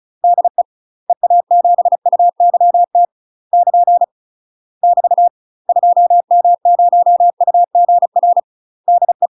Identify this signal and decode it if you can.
Morse code, heard as a keyed tone, about 34 wpm, 700 Hz. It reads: DE EA7UYT <KN> <BT> 2M0UGR DE
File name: UK MASTER QSOs TRAINING / 35WPM / UK MASTER QSOs 0005